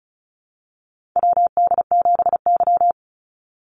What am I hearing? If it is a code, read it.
Morse code, heard as a keyed tone, about 36 wpm, 700 Hz. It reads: WD7Y
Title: NA SOTA Chaser Callsigns 35wpm